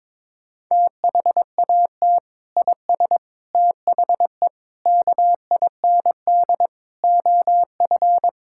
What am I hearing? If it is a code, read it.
Morse code, heard as a keyed tone, about 22 wpm, 700 Hz. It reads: THAT IS THE KIND OF
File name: practice / 22wpm / Sentences with 3-Word Phrases - 22wpm